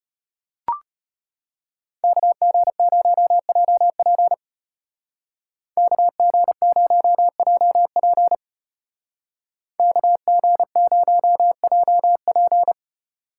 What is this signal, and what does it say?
Morse code, heard as a keyed tone, about 30 wpm, 700 Hz. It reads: KG0JP KG0JP KG0JP
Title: Speed Racing - USA 2x2 Call Signs 25wpm